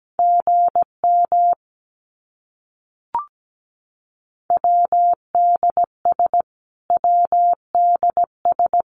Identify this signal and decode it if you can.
Morse code, heard as a keyed tone, about 17 wpm, 700 Hz. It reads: GM WDS WDS
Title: Abbreviations Repeated 3x - 17wpm